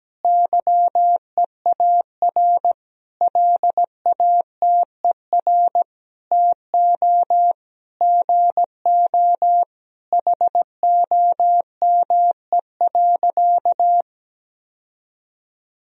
Morse code, heard as a keyed tone, about 17 wpm, 700 Hz. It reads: YEAR LATER TO GO HOME.
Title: Sentences from Top 1000 Words - Review Part 4 - 17wpm